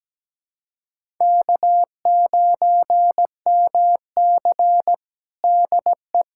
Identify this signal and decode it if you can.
Morse code, heard as a keyed tone, about 17 wpm, 700 Hz. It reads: K9MC DE
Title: USA Callsign DE Callsign - 4 Characters - 17wpm